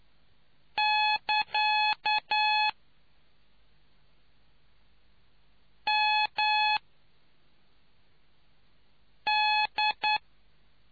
Morse code, heard as a keyed tone, about 9 wpm, 850 Hz. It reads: <KA> M D